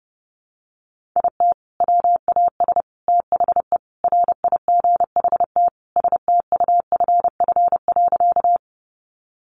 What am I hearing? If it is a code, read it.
Morse code, heard as a keyed tone, about 30 wpm, 700 Hz. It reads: IT WAS THE RIGHT STUFF.
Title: Sentences from Top 1400 Words 30wpm